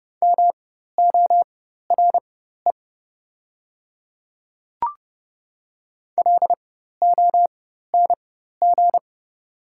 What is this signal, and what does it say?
Morse code, heard as a keyed tone, about 30 wpm, spaced as for 14 wpm, 700 Hz. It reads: MORE LONG
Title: 4x Character Spacing - Top 100 Words - 30wpm